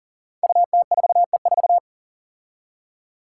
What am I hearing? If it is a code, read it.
Morse code, heard as a keyed tone, about 40 wpm, 700 Hz. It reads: UT4E4
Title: Five Challenging Letter-Number Combinations 40wpm